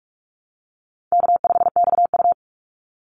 Morse code, heard as a keyed tone, about 44 wpm, 700 Hz. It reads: K5XU
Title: Easy Call Signs Repeated 2x - 45wpm